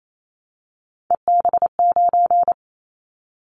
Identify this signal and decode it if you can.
Morse code, heard as a keyed tone, about 28 wpm, 700 Hz. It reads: EB9
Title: Callsign Prefixes for Top 100 DXCC - 28wpm